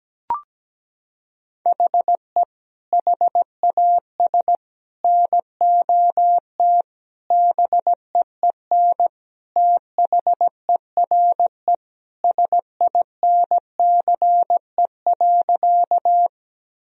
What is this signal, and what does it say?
Morse code, heard as a keyed tone, about 17 wpm, 700 Hz. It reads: HE HAS NOT BEEN THERE SINCE.
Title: Sentences from Top 1000 Words - Review Part 1 - 17wpm